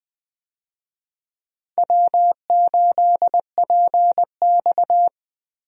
Morse code, heard as a keyed tone, about 20 wpm, 700 Hz. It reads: W8PX
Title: USA Call Signs - 4 Characters - 20wpm